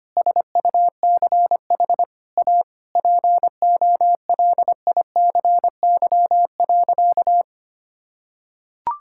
Morse code, heard as a keyed tone, about 25 wpm, 700 Hz. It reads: SUCH A POLICY.
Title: Sentences from Top 1000 Words - Review Part 6 - 25wpm